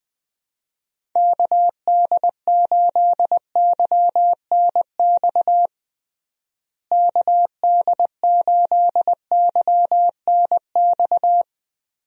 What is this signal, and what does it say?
Morse code, heard as a keyed tone, about 20 wpm, 700 Hz. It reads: KD8YNX KD8YNX
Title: USA Call Signs - 6 Characters - Repeated 2x - 20wpm